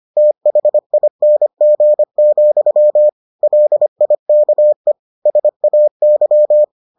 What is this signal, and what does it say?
Morse code, heard as a keyed tone, about 25 wpm, 600 Hz. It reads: THING, LIKE SAY